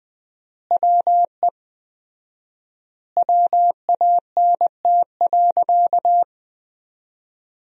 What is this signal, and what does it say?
Morse code, heard as a keyed tone, about 20 wpm, 700 Hz. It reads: WE WANT.
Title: 4x Word-Spacing - Sentences from Top 200 Words - 20wpm